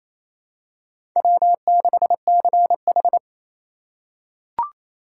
Morse code, heard as a keyed tone, about 27 wpm, 700 Hz. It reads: W6CH
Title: USA Call Signs - 4 Characters - 2x Character Spacing - 28wpm